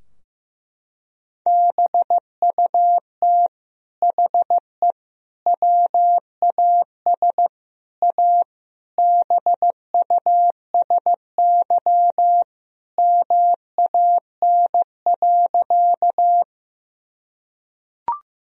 Morse code, heard as a keyed tone, about 15 wpm, 700 Hz. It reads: BUT HE WAS A BUSY MAN.
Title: Sentences from Top 1000 Words - Review Part 3 - 15wpm